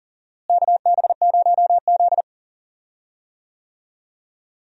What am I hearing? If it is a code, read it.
Morse code, heard as a keyed tone, about 40 wpm, 700 Hz. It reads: KB0Z
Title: USA Call Signs - 4 Characters - Repeated 2x - 40wpm